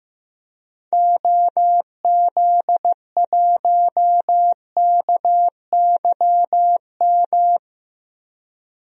Morse code, heard as a keyed tone, about 15 wpm, 700 Hz. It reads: OZ1KYM
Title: DXCC-100 Call Signs - 6 Characters - 15wpm